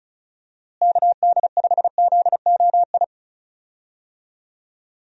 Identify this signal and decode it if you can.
Morse code, heard as a keyed tone, about 36 wpm, 700 Hz. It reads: KD5ZOI
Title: USA Call Signs - 6 Characters - Repeated 3x - 35wpm